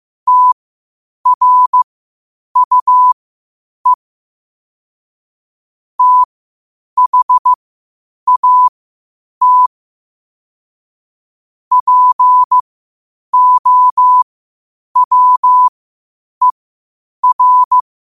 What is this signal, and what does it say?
Morse code, heard as a keyed tone, about 15 wpm, spaced as for 8 wpm, 1000 Hz. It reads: TRUE THAT POWER
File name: qotdfiles / 8wpm